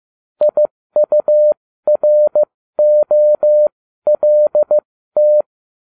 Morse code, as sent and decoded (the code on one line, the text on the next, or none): .. ..- .-. --- .-.. -
IUROLT